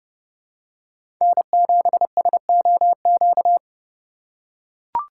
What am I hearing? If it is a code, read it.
Morse code, heard as a keyed tone, about 32 wpm, 700 Hz. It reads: N7SOQ
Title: USA Call Signs - 5 Characters - 3x Character Spacing - 30wpm